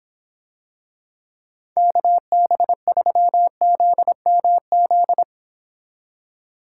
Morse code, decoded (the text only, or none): KB3ZMZ